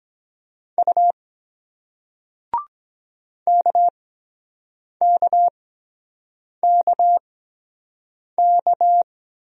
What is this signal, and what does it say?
Morse code, heard as a keyed tone, about 23 wpm, spaced as for 7 wpm, 700 Hz. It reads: U KKKK